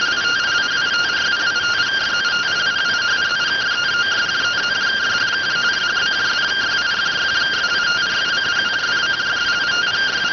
Two-tone teleprinter signal, about 170 Hz shift.